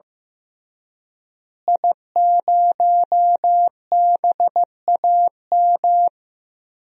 Morse code, heard as a keyed tone, about 15 wpm, 700 Hz. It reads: I0BAM
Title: DXCC-100 Call Signs - 5 Characters - Repeated 3x - 15wpm